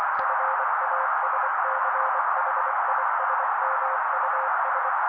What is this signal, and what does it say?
Morse code, heard as a keyed tone, about 24 wpm, 530 Hz. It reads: FASCHISMUS
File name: daily / 23wpm noise